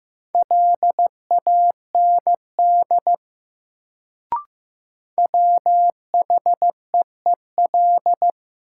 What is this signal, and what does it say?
Morse code, heard as a keyed tone, about 16 wpm, 700 Hz. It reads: LAND WHEEL
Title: Sentences from Top 500 Words - New Word Drill - 15wpm